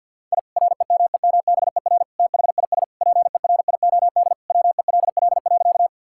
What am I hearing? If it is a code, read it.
Morse code, heard as a keyed tone, about 46 wpm, 700 Hz. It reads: I REMEMBER THIS PERIOD WELL.
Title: Sentences from Top 1000 Words - Review Part 5 - 50wpm